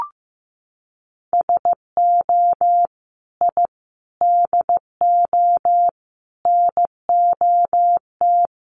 Morse code, heard as a keyed tone, about 15 wpm, 700 Hz. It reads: SO I DO NOT
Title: Sets of 5 Words 15wpm